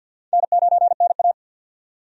Morse code, heard as a keyed tone, about 51 wpm, 700 Hz. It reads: N9NA
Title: USA Call Signs - 4 Characters - 50wpm